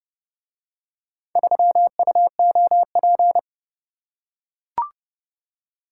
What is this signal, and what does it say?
Morse code, heard as a keyed tone, about 30 wpm, 700 Hz. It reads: 3UOP